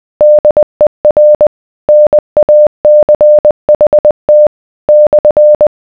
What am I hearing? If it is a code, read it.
Morse code, heard as a keyed tone, about 20 wpm, 600 Hz. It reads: DER NACHT /